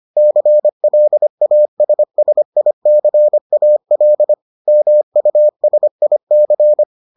Morse code, heard as a keyed tone, about 25 wpm, 600 Hz. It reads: CLASSICAL MUSIC